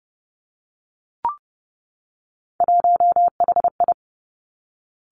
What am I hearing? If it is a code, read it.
Morse code, heard as a keyed tone, about 30 wpm, 700 Hz. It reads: 1HI
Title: Three Challenging Letter-Number Combinations 30wpm